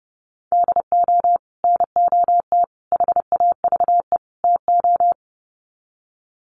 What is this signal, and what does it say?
Morse code, heard as a keyed tone, about 30 wpm, 700 Hz. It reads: DO NOT HAVE TO
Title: Sets of 4 Words 30wpm